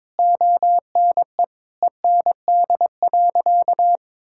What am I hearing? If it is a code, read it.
Morse code, heard as a keyed tone, about 21 wpm, 700 Hz. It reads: ONE END.
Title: Sentences from Top 1600 Words 22wpm